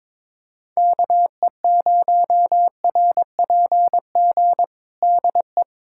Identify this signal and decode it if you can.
Morse code, heard as a keyed tone, about 22 wpm, 700 Hz. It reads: KE0RPG DE
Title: USA Callsign DE Callsign - 6 Characters - 22wpm